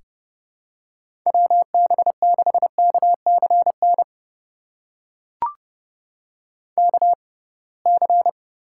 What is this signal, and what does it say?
Morse code, heard as a keyed tone, about 30 wpm, spaced as for 23 wpm, 700 Hz. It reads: WB6KCN KC
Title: USA Call Signs - 6 Characters - 6x Character Spacing - 30wpm